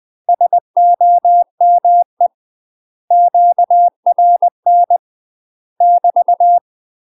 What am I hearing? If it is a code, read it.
Morse code, heard as a keyed tone, about 20 wpm, 700 Hz. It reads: SOME QRN <BT>
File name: UK MASTER QSOs TRAINING / 20WPM / UK MASTER QSOs 0017